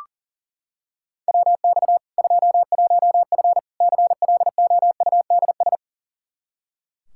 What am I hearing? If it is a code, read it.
Morse code, heard as a keyed tone, about 37 wpm, 700 Hz. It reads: WX 21F CLOUDS
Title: K1USN SST QSO Practice 40wpm